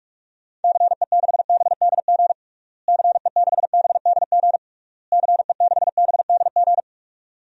Morse code, heard as a keyed tone, about 46 wpm, 700 Hz. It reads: CE6BDG CE6BDG CE6BDG
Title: DXCC-100 Call Signs - 6 Characters - Repeated 3x - 45wpm